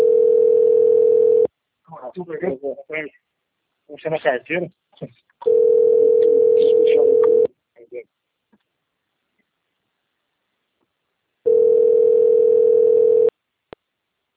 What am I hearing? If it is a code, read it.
Ringback tone (North America), 440 + 480 Hz, with speech.
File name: out- - - - - .72973